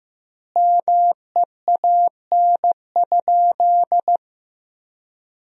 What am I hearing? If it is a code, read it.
Morse code, heard as a keyed tone, about 15 wpm, 700 Hz. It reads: MEAN?
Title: 2x Word-Spacing - Sentences from Top 200 Words - 15wpm